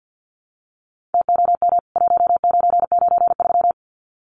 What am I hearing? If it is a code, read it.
Morse code, heard as a keyed tone, about 48 wpm, 700 Hz. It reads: TOM 1993
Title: CWT Exchange 50wpm